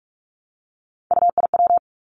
Morse code, heard as a keyed tone, about 46 wpm, 700 Hz. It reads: UIW
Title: Three Challenging Letter-Number Combinations 45wpm